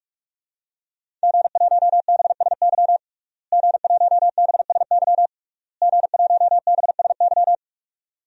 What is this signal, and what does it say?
Morse code, heard as a keyed tone, about 42 wpm, 700 Hz. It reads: G1BSY G1BSY G1BSY
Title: DXCC-100 Call Signs - 5 Characters - Repeated 3x - 45wpm